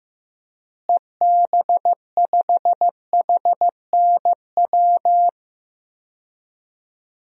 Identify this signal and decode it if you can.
Morse code, heard as a keyed tone, about 16 wpm, 700 Hz. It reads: EB5HNW
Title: DXCC-100 Call Signs - 6 Characters - Repeated 2x - 15wpm